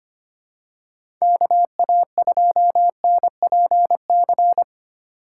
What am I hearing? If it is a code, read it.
Morse code, heard as a keyed tone, about 25 wpm, 700 Hz. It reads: KA2NPC
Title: USA Call Signs - 6 Characters - 4x Character Spacing - 25wpm